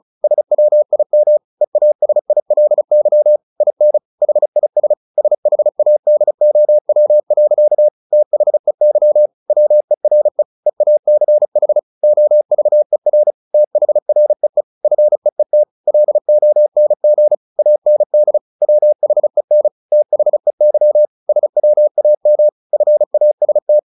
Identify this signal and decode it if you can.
Morse code, heard as a keyed tone, about 34 wpm, 600 Hz. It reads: SWIM EASILY IN HIS SHADOW. THEY WERE EACH OVER THREE FEET LONG AND WHEN THEY SWAM FAST
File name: the old man and the sea 2 35wpm